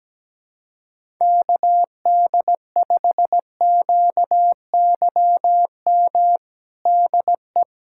Morse code, heard as a keyed tone, about 17 wpm, 700 Hz. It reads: KD5QYM DE
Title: USA Callsign DE Callsign - 6 Characters - 17wpm